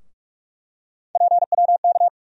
Morse code, heard as a keyed tone, about 46 wpm, 700 Hz. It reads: PWK
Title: USA Call Sign Postfixes - 3 Letters 45wpm